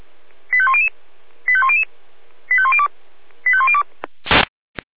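Five-tone selective call (ZVEI), heard as quick runs of five tones, tones of about 70 ms. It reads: 85100 85100 85102 85102
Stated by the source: Five-tone ZVEI selective call 85100 received on an SDR.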